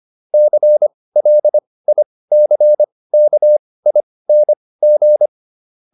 Morse code, heard as a keyed tone, about 25 wpm, spaced as for 18 wpm, 600 Hz. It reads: CLICKING